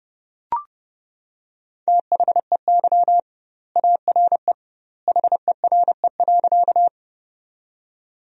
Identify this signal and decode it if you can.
Morse code, heard as a keyed tone, about 30 wpm, 700 Hz. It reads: THEY ARE HERE.
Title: 2x Word-Spacing - Sentences from Top 200 Words - 30wpm